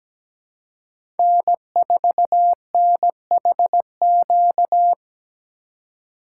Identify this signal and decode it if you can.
Morse code, heard as a keyed tone, about 17 wpm, 700 Hz. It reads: N4NHQ
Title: USA Call Signs - 5 Characters - 2x Character Spacing - 17wpm